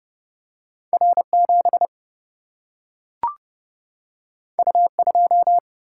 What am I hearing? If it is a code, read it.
Morse code, heard as a keyed tone, about 30 wpm, 700 Hz. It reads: R7 U2
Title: Single-Letter Single-Number 30wpm